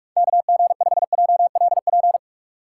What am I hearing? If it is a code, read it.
Morse code, heard as a keyed tone, about 44 wpm, 700 Hz. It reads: KG5JLP
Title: USA Call Signs - 3x Character Spacing - 45wpm